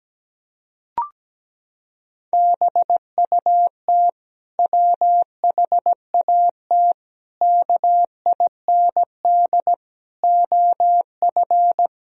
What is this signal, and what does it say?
Morse code, heard as a keyed tone, about 17 wpm, 700 Hz. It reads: BUT WHAT KIND OF
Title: Sentences from Top 1000 Words - Review Part 1 - 17wpm